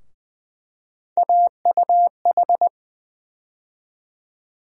Morse code, heard as a keyed tone, about 20 wpm, 700 Hz. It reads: AUH
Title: USA Call Sign Postfixes - 3 Letters 20wpm